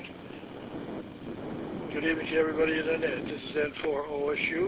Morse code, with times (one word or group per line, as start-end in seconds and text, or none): none